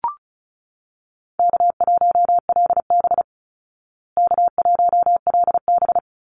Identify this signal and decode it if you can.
Morse code, heard as a keyed tone, about 34 wpm, 700 Hz. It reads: K1LB K1LB
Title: Easy Call Signs Repeated 2x - 35wpm